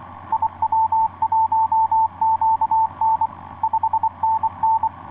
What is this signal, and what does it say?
Morse code, heard as a keyed tone, about 22 wpm, 880 Hz, with radio noise. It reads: IW1QN 5NN
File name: 6w-gm4fdm yo9fp etc 12m cw 04feb2010 1154